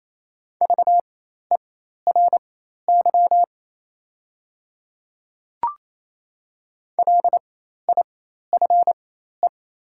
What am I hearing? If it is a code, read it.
Morse code, heard as a keyed tone, about 28 wpm, spaced as for 13 wpm, 700 Hz. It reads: VERY LIFE
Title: 4x Character Spacing - Top 300 Words - 28wpm